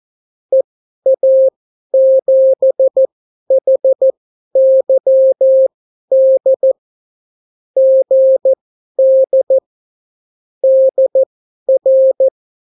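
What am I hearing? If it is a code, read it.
Morse code, heard as a keyed tone, about 14 wpm, spaced as for 11 wpm, 530 Hz. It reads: EA7HYD GD DR
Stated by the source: Example Spanish CW QSO from a Morse learning book, 14 wpm characters with 8 wpm spacing.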